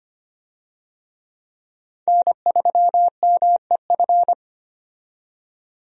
Morse code, heard as a keyed tone, about 25 wpm, 700 Hz. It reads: N3MEF
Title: USA Call Signs - 5 Characters - 25wpm